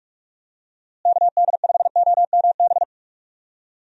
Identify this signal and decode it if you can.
Morse code, heard as a keyed tone, about 46 wpm, 700 Hz. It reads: KD5OMB
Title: USA Call Signs - 6 Characters - Repeated 3x - 45wpm